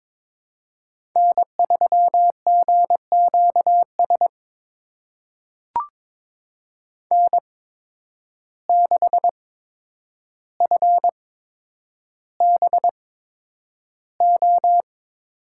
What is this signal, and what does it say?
Morse code, heard as a keyed tone, about 22 wpm, spaced as for 9 wpm, 700 Hz. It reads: N3GQS N6FBO
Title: USA Call Signs - 5 Characters - 8x Character Spacing - 22wpm